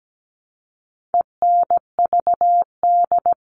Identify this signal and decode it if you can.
Morse code, heard as a keyed tone, about 17 wpm, 700 Hz. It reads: ENVD
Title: All Cut Numbers - 4-Digit 17wpm